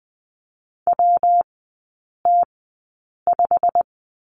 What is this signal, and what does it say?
Morse code, heard as a keyed tone, about 20 wpm, spaced as for 8 wpm, 690 Hz. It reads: WT5
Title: 145 Lesson 24e - Letter y - Three Characters 20wpm